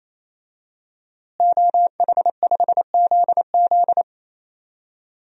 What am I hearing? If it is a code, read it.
Morse code, heard as a keyed tone, about 28 wpm, 700 Hz. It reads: OH5ZZ
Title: DXCC-100 Call Signs - 5 Characters - Repeated 3x - 28wpm